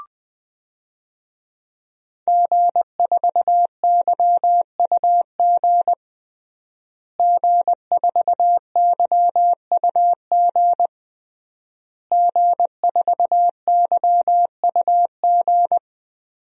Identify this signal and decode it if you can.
Morse code, heard as a keyed tone, about 20 wpm, 700 Hz. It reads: G4YUG G4YUG G4YUG
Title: DXCC-100 Call Signs - 5 Characters - Repeated 3x - 20wpm